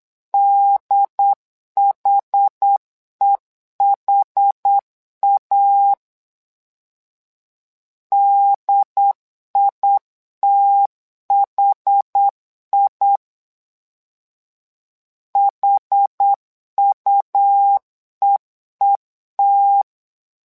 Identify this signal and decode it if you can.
Morse code, heard as a keyed tone, about 8 wpm, 800 Hz. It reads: DHEHA DITHI HUEET
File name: LEZIONI / 2 UHDN / 2 UHD 8wpm 2258